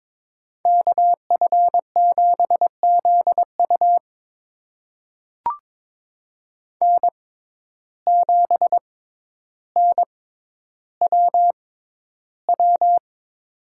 Morse code, heard as a keyed tone, about 22 wpm, spaced as for 11 wpm, 700 Hz. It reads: KF7ZU N7NWW
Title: USA Call Signs - 5 Characters - 6x Character Spacing - 22wpm